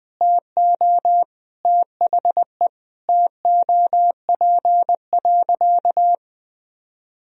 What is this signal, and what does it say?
Morse code, heard as a keyed tone, about 21 wpm, 700 Hz. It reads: TO THE TOP.
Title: Sentences from Top 1000 Words - Review Part 2 - 20wpm